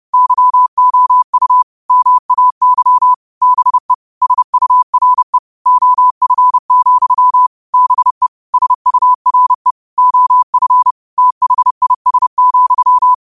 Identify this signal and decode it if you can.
Morse code, heard as a keyed tone, about 30 wpm, 1000 Hz. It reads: YOU MAY BE SURE OF, BE SURE OF THIS,
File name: qotdfiles / 30wpm